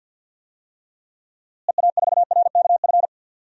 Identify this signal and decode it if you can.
Morse code, heard as a keyed tone, about 52 wpm, 700 Hz. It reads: EA4RKF
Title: DXCC-100 Call Signs - 6 Characters - Repeated 3x - 50wpm